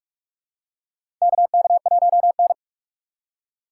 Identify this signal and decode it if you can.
Morse code, heard as a keyed tone, about 47 wpm, 700 Hz. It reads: KK1N
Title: USA Call Signs - 4 Characters - 3x Character Spacing - 45wpm